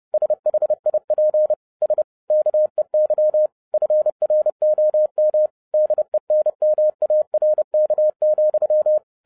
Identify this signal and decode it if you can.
Morse code, heard as a keyed tone, about 30 wpm, 610 Hz. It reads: SHIP S KEY FROM DENMARK,
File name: 2011-07-mp3 / n20 GREETINGS FROM SINGAPORE 30wpm